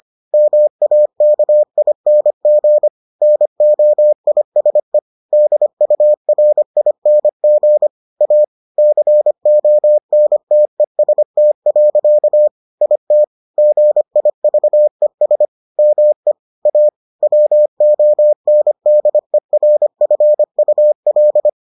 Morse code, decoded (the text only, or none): MAKING NOISE DURING A CONTEST. IT GIVES ME A WONDERFUL